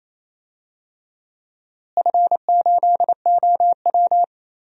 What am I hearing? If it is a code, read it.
Morse code, heard as a keyed tone, about 28 wpm, 700 Hz. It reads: F8OW
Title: DXCC-100 Call Signs - 4 Characters - Repeated 3x - 28wpm